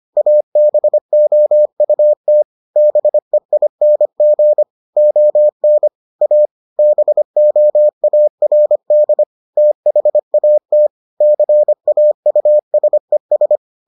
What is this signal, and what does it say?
Morse code, heard as a keyed tone, about 25 wpm, 600 Hz. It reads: ABOUT BEING ON A BOARD THAT CAUSES